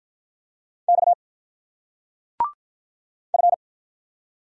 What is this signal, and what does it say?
Morse code, heard as a keyed tone, about 43 wpm, 700 Hz. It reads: X F
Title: Speed-Racing Single Letters 35wpm